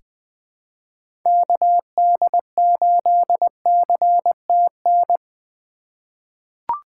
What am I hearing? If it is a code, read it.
Morse code, heard as a keyed tone, about 21 wpm, 700 Hz. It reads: KD8CTN
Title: USA Call Signs - 3x Character Spacing - 20wpm